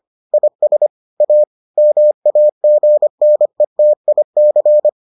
Morse code, heard as a keyed tone, about 25 wpm, 600 Hz. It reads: IS A MAGNETIC